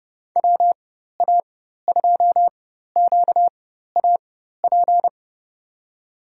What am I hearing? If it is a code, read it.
Morse code, heard as a keyed tone, about 30 wpm, spaced as for 14 wpm, 700 Hz. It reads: WA2QAP
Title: USA Call Signs - 6 Characters - 4x Character Spacing - 30wpm